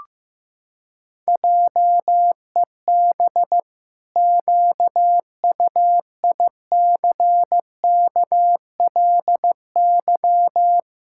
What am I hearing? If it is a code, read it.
Morse code, heard as a keyed tone, about 15 wpm, 700 Hz. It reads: JEB QUICKLY
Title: Pangram Challenge v2 - 15wpm